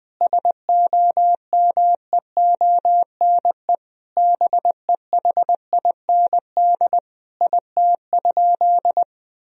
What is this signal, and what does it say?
Morse code, heard as a keyed tone, about 20 wpm, 700 Hz. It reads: SOMEONE BEHIND IT?
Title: Sentences from Top 1000 Words - Review Part 4 - 20wpm